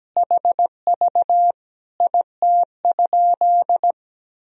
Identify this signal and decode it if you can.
Morse code, heard as a keyed tone, about 17 wpm, 700 Hz. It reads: HV IT?
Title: Sentences from Top 100 Words and Abbreviations 17wpm